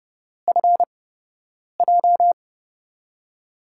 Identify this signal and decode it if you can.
Morse code, heard as a keyed tone, about 30 wpm, spaced as for 8 wpm, 700 Hz. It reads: FJ
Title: USA Call Signs - 5 Characters - 8x Character Spacing - 30wpm